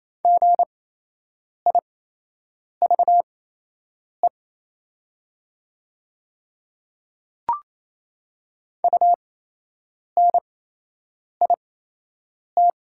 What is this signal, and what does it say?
Morse code, heard as a keyed tone, about 28 wpm, spaced as for 8 wpm, 700 Hz. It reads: GIVE UNIT
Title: 8x Character Spacing - Top 500 Words - 28wpm